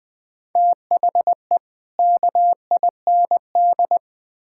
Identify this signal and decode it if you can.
Morse code, heard as a keyed tone, about 20 wpm, 700 Hz. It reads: THE KIND
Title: Build A Phrase - 3-Word Phrases - 20wpm